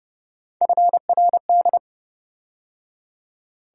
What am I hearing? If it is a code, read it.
Morse code, heard as a keyed tone, about 28 wpm, 700 Hz. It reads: FRD